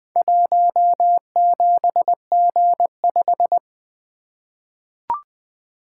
Morse code, heard as a keyed tone, about 20 wpm, 700 Hz. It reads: 17G5
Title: Four Letter-Number Combinations - 20wpm